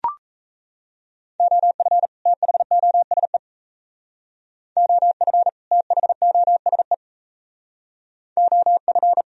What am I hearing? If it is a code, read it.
Morse code, heard as a keyed tone, about 33 wpm, 700 Hz. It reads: OF THOSE OF THOSE OF
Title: Speed-Racing - Sets of 2 Words 28wpm